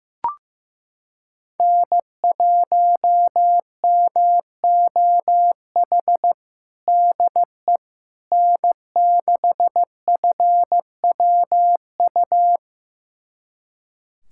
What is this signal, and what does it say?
Morse code, heard as a keyed tone, about 15 wpm, 700 Hz. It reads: N1MOH DE N6FWU
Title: USA Callsign DE Callsign - 5 Characters - 15wpm